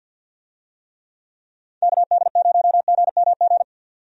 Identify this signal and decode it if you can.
Morse code, heard as a keyed tone, about 52 wpm, 700 Hz. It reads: KD0GMG
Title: USA Call Signs - 6 Characters - 50wpm